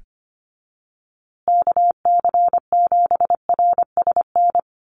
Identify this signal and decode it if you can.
Morse code, heard as a keyed tone, about 25 wpm, 700 Hz. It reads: KC7RSN
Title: USA Call Signs Repeated 2x 25wpm